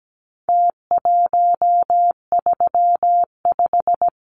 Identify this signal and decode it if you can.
Morse code, heard as a keyed tone, about 17 wpm, 700 Hz. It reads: T135
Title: Numbers - Contest Style - 4-Digit 17wpm